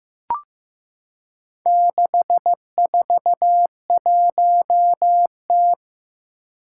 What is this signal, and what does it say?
Morse code, heard as a keyed tone, about 16 wpm, 700 Hz. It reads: 641T
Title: Four Challenging Letter-Number Combinations 15wpm